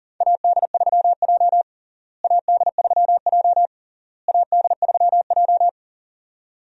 Morse code, heard as a keyed tone, about 40 wpm, 700 Hz. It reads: AD3J AD3J AD3J
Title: USA Call Signs - 4 Characters - Repeated 3x - 40wpm